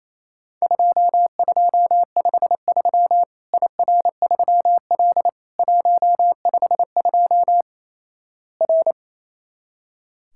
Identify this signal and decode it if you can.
Morse code, heard as a keyed tone, about 28 wpm, 700 Hz. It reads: 2253 IR3L 152 R
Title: Worked All Europe DX Contest - QTC Exchange - 28wpm